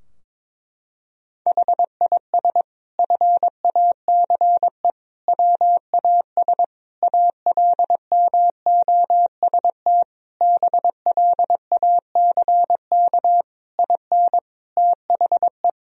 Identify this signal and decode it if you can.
Morse code, heard as a keyed tone, about 22 wpm, 700 Hz. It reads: HIS FACE WAS ALMOST BLACK IN THE
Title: Sentences from Top 1000 Words - Review Part 4 - 22wpm